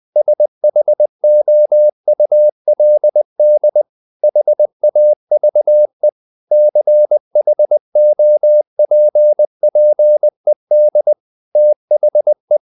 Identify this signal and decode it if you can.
Morse code, heard as a keyed tone, about 20 wpm, 600 Hz. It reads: SHOULD HAVE CHOPPED THE
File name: the old man and the sea part 2 20wpm